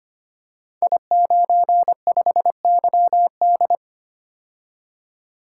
Morse code, heard as a keyed tone, about 25 wpm, 700 Hz. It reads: I95YD
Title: Code Groups - 5 Letter:Numbers - 25wpm